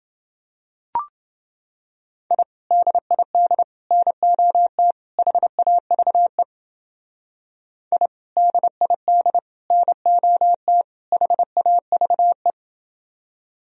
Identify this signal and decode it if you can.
Morse code, heard as a keyed tone, about 29 wpm, 700 Hz. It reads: I DID NOT HAVE I DID NOT HAVE